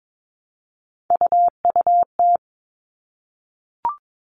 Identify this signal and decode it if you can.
Morse code, heard as a keyed tone, about 22 wpm, 700 Hz. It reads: UUT
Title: All Cut Numbers - 3-Digit 22wpm